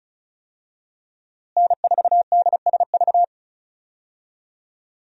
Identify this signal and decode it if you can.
Morse code, heard as a keyed tone, about 34 wpm, 700 Hz. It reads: N4DSV
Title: USA Call Signs - 5 Characters - Repeated 2x - 35wpm